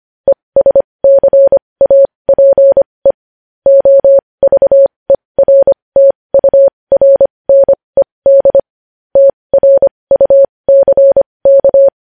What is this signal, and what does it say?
Morse code, heard as a keyed tone, about 25 wpm, spaced as for 20 wpm, 550 Hz. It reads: ESCAPE OVERTURNED TRUCK
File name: podcast / News headlines-1x-25x20